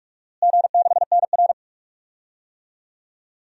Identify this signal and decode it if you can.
Morse code, heard as a keyed tone, about 46 wpm, 700 Hz. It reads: G6NR
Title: DXCC-100 Call Signs - 4 Characters - Repeated 3x - 45wpm